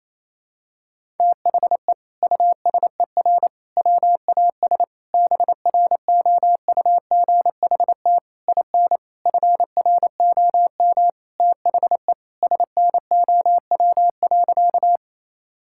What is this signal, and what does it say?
Morse code, heard as a keyed tone, about 28 wpm, 700 Hz. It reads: THE USER WAS BROUGHT IN FROM THE SNOW.
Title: Sentences from Top 1000 Words - Review Part 5 - 28wpm